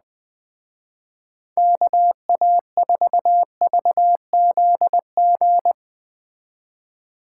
Morse code, heard as a keyed tone, about 20 wpm, 700 Hz. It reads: KA4VZG
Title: Speed Racing - USA Call Signs 17wpm